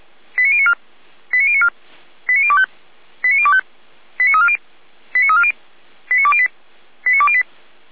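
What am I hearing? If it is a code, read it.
Five-tone selective call (ZVEI), heard as quick runs of five tones, tones of about 70 ms. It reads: 89094 89094 89025 89025 89340 89340 89208 89208
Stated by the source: ZVEI five-tone call received by SDR.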